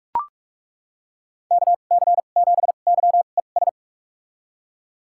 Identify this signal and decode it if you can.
Morse code, heard as a keyed tone, about 42 wpm, spaced as for 31 wpm, 700 Hz. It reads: KC7YES
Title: USA Call Signs - 6 Characters - 2x Character Spacing - 45wpm